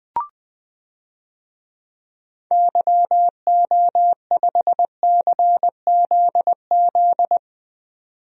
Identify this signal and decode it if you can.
Morse code, heard as a keyed tone, about 20 wpm, 700 Hz. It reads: YO5CZZ
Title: DXCC-100 Call Signs - 6 Characters - 20wpm